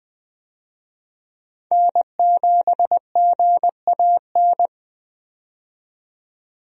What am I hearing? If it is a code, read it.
Morse code, heard as a keyed tone, about 20 wpm, 700 Hz. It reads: N7GAN
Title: USA Call Signs - 5 Characters - 20wpm